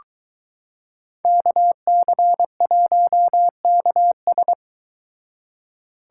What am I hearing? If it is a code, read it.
Morse code, heard as a keyed tone, about 22 wpm, 700 Hz. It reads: KC1KS